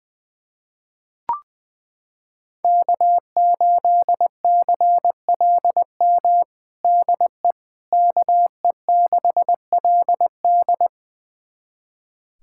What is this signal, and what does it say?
Morse code, heard as a keyed tone, about 21 wpm, 700 Hz. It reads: K8CLM DE KE6LD
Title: USA Callsign DE Callsign - 5 Characters - 20wpm